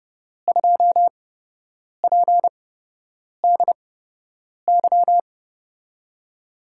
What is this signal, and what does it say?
Morse code, heard as a keyed tone, about 32 wpm, spaced as for 8 wpm, 700 Hz. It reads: 2PDY
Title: USA Call Signs - 5 Characters - 8x Character Spacing - 30wpm